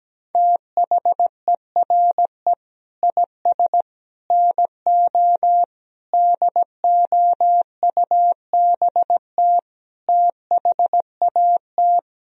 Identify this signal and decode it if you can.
Morse code, heard as a keyed tone, about 17 wpm, 700 Hz. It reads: THERE IS NO DOUBT THAT
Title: Speed-Racing - Sets of 5 Words 17wpm